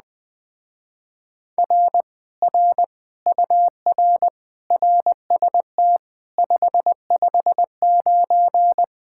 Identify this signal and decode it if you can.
Morse code, heard as a keyed tone, about 21 wpm, 700 Hz. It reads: R R UR RST 559